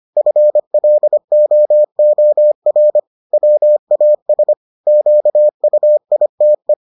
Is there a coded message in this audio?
Yes: Morse code from 0.2 to 6.7 s, heard as a keyed tone, about 25 wpm, 600 Hz. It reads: FLOOR WAS QUITE